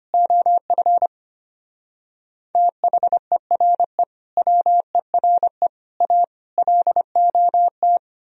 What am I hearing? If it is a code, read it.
Morse code, heard as a keyed tone, about 31 wpm, 700 Hz. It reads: OF THERE WERE A LOT
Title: Speed-Racing - Sets of 5 Words 25wpm